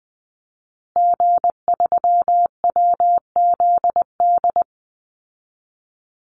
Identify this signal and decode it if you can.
Morse code, heard as a keyed tone, about 20 wpm, 700 Hz. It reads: G3WZD
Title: Call Signs Repeated 2x - 20wpm